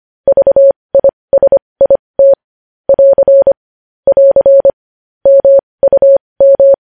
Morse code, heard as a keyed tone, about 25 wpm, spaced as for 20 wpm, 550 Hz. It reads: VISIT <AR> <AR> MUM
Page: news headlines in morse code at 20 wpm